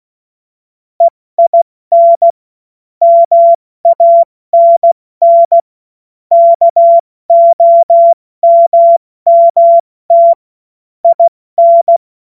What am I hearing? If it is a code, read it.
Morse code, heard as a keyed tone, about 16 wpm, 680 Hz. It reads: EIN MANN KOMMT IN
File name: Morsekurs / Witze / Witz 07. Juni 2022 70 F80